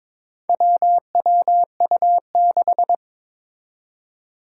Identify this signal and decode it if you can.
Morse code, heard as a keyed tone, about 22 wpm, 700 Hz. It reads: WWU6
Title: Four Challenging Letter-Number Combinations 22wpm